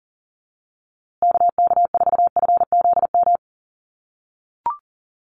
Morse code, heard as a keyed tone, about 40 wpm, 700 Hz. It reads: KK4FZM
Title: USA Call Signs Repeated 2x 40wpm